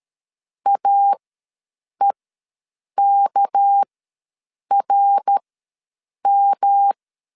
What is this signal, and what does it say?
Morse code, heard as a keyed tone, about 13 wpm, spaced as for 7 wpm, 800 Hz. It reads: AEKRM
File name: LEZIONI / 6 PKX0 / 6 PKX ws 12wpm 3106